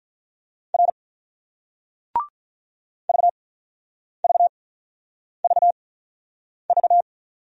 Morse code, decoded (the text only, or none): R VVVV